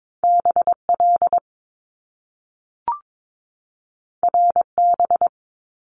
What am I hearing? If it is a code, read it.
Morse code, heard as a keyed tone, about 22 wpm, 700 Hz. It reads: BL RB
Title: Top Two Letter Combinations in Words 22wpm